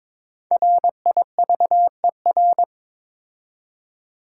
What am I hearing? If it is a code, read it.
Morse code, heard as a keyed tone, about 22 wpm, 700 Hz. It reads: RIVER
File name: 5 Letter Words - First 3 Letters Given - 22wpm